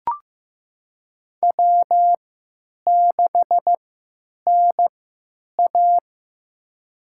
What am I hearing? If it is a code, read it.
Morse code, heard as a keyed tone, about 15 wpm, spaced as for 9 wpm, 700 Hz. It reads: W6NA
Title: USA Call Signs - 4 Characters - 3x Character Spacing - 15wpm